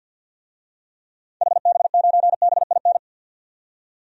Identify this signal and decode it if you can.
Morse code, heard as a keyed tone, about 49 wpm, 700 Hz. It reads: HB9BIN
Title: DXCC-100 Call Signs - 6 Characters - 50wpm